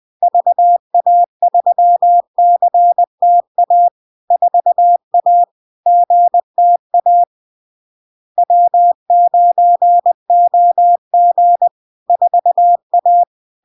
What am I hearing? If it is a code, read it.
Morse code, heard as a keyed tone, about 20 wpm, 700 Hz. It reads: VA3CTA 4A GTA W9OG 4A
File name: LARK-FD-2022-Log-in-CW-20wpm-700-Hz